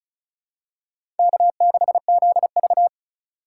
Morse code, heard as a keyed tone, about 36 wpm, 700 Hz. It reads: K6ZV
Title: USA Call Signs - 2x Character Spacing - 35wpm